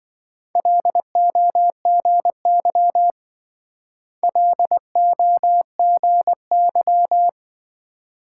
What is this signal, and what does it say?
Morse code, heard as a keyed tone, about 27 wpm, 700 Hz. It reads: LOGY LOGY